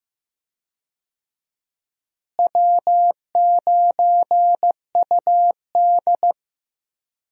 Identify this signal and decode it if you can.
Morse code, heard as a keyed tone, about 15 wpm, 700 Hz. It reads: W9UD
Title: USA Call Signs - 4 Characters - Repeated 3x - 15wpm